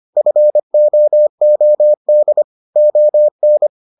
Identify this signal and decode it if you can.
Morse code, heard as a keyed tone, about 25 wpm, 600 Hz. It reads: FOOD ON